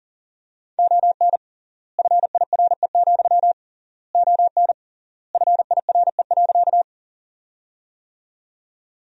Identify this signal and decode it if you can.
Morse code, heard as a keyed tone, about 40 wpm, 700 Hz. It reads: ON FIRE, ON FIRE.
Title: 3x Word-Spacing - Sentences from Top 400 Words - 40wpm